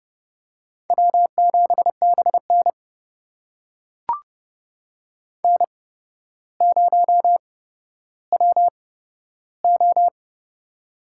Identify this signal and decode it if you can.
Morse code, heard as a keyed tone, about 30 wpm, spaced as for 13 wpm, 700 Hz. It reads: W7BN N0WO
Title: USA Call Signs - 4 Characters - 8x Character Spacing - 30wpm